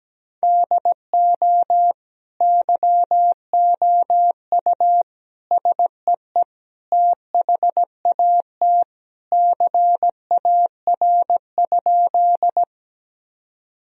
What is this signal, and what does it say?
Morse code, heard as a keyed tone, about 17 wpm, 700 Hz. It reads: DO YOU SEE THAT CAR?
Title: Sentences from Top 1000 Words - Review Part 1 - 17wpm